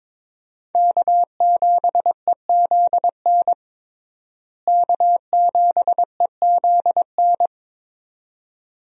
Morse code, heard as a keyed tone, about 22 wpm, 700 Hz. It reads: K7EZN K7EZN
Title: USA Call Signs - 5 Characters - Repeated 2x - 22wpm